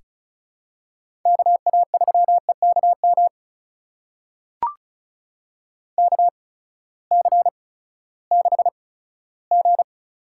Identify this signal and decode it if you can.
Morse code, heard as a keyed tone, about 34 wpm, spaced as for 17 wpm, 700 Hz. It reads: KA3EKM KC6G
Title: USA Call Signs - 6 Characters - 8x Character Spacing - 35wpm